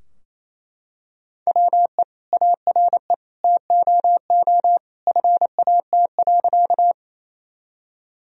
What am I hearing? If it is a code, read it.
Morse code, heard as a keyed tone, about 28 wpm, 700 Hz. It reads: WE ARE TOO FAT.
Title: Sentences from Top 1000 Words - Review Part 4 - 28wpm